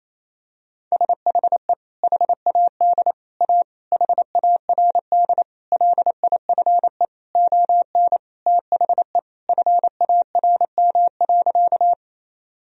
Morse code, heard as a keyed tone, about 28 wpm, 700 Hz. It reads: SHE HAD A HARD LIFE ON THE FARM.
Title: Sentences from Top 1000 Words - Review Part 2 - 28wpm